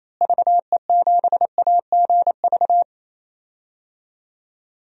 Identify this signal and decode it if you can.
Morse code, heard as a keyed tone, about 28 wpm, 700 Hz. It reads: VE7AGV
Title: DXCC-100 Call Signs - 6 Characters - Repeated 3x - 28wpm